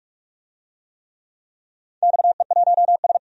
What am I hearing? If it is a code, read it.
Morse code, heard as a keyed tone, about 44 wpm, 700 Hz. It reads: XE1S
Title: DXCC-100 Call Signs - 4 Characters - 45wpm